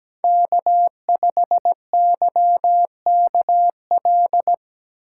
Morse code, heard as a keyed tone, about 17 wpm, 700 Hz. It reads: K5YKL
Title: USA Call Signs - 5 Characters - 6x Character Spacing - 17wpm